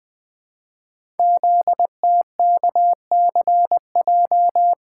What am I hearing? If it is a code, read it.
Morse code, heard as a keyed tone, about 20 wpm, 700 Hz. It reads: ZTKCJ
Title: Code Groups - 5 Letters - 20wpm